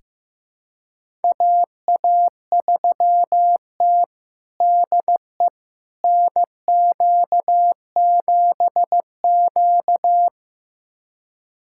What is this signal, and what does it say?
Morse code, heard as a keyed tone, about 15 wpm, 700 Hz. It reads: AA3T DE NQ7Q
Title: USA Callsign DE Callsign - 4 Characters - 15wpm